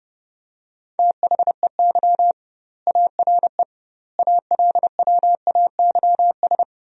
Morse code, heard as a keyed tone, about 30 wpm, 700 Hz. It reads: THEY ARE ALWAYS
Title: 2x Word-Spacing - Sentences from Top 300 Words - 30wpm